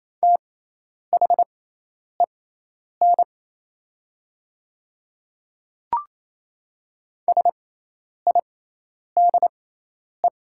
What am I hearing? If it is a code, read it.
Morse code, heard as a keyed tone, about 27 wpm, spaced as for 10 wpm, 700 Hz. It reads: THEN SIDE
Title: 6x Character Spacing - Top 100 Words - 28wpm